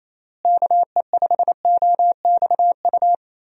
Morse code, heard as a keyed tone, about 28 wpm, 700 Hz. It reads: KE5OXU
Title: USA Call Signs - 6 Characters - Repeated 3x - 28wpm